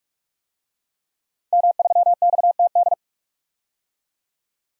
Morse code, heard as a keyed tone, about 46 wpm, 700 Hz. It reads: M3XTD
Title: DXCC-100 Call Signs - 5 Characters - 45wpm